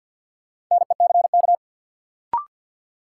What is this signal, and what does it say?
Morse code, heard as a keyed tone, about 51 wpm, 700 Hz. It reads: NECK
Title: Sets of 5 Words Encore - New Word Drill - 50wpm